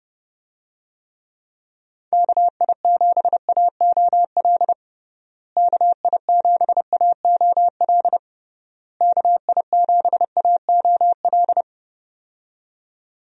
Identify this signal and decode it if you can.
Morse code, heard as a keyed tone, about 30 wpm, 700 Hz. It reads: KI7AOL KI7AOL KI7AOL
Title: USA Call Signs - 6 Characters - Repeated 3x - 30wpm